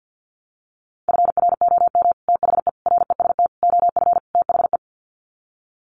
Morse code, heard as a keyed tone, about 50 wpm, 700 Hz. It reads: FROM THE REST OF THE
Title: Sets of 5 Words 50wpm